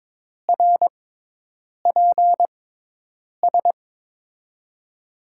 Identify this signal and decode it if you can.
Morse code, heard as a keyed tone, about 22 wpm, spaced as for 8 wpm, 700 Hz. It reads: RPS
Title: USA Call Signs - 6 Characters - 6x Character Spacing - 22wpm